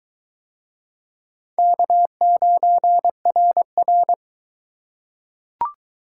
Morse code, heard as a keyed tone, about 22 wpm, 700 Hz. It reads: K9RR